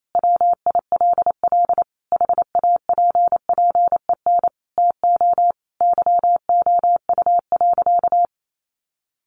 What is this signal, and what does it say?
Morse code, heard as a keyed tone, about 28 wpm, 700 Hz. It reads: WILL HAPPEN TO YOU.
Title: Sentences from Top 500 Words 28wpm